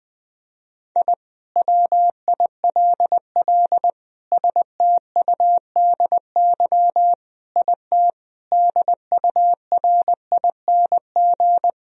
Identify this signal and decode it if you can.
Morse code, heard as a keyed tone, about 20 wpm, 700 Hz. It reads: I WILL STUDY IT DURING
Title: Sentences from Top 1000 Words - Review Part 2 - 20wpm